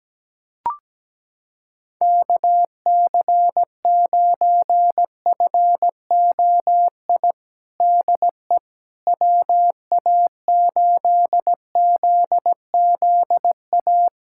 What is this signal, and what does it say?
Morse code, heard as a keyed tone, about 17 wpm, 700 Hz. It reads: KC9FOI DE WA8ZZA
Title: USA Callsign DE Callsign - 6 Characters - 17wpm